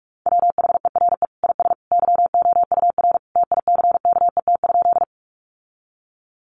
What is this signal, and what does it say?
Morse code, heard as a keyed tone, about 44 wpm, 700 Hz. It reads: WHERE IS YOUR TICKET?
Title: Sentences from Top 1400 Words 45wpm